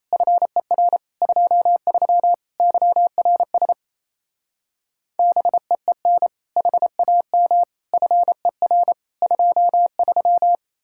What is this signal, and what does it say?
Morse code, heard as a keyed tone, about 37 wpm, 700 Hz. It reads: FER 23 YRS BEEN HAM FER 23